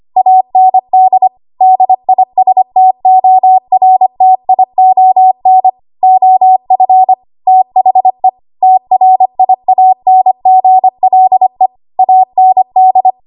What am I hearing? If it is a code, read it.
Morse code, heard as a keyed tone, about 25 wpm, 750 Hz. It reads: AND DISTORTION OF THE TRIANGLE AND
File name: Archive / 25 WPM / 25WPM